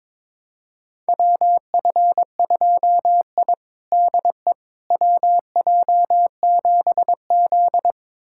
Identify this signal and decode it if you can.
Morse code, heard as a keyed tone, about 22 wpm, 700 Hz. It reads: WF2I DE WJ7Z
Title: USA Callsign DE Callsign - 4 Characters - 22wpm